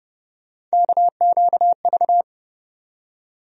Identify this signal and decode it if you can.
Morse code, heard as a keyed tone, about 31 wpm, 700 Hz. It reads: KQV
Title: USA Call Sign Postfixes - 3 Letters 30wpm